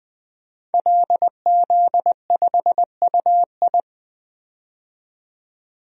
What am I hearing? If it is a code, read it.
Morse code, heard as a keyed tone, about 20 wpm, 700 Hz. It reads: LZ5UI
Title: DXCC-100 Call Signs - 5 Characters - 20wpm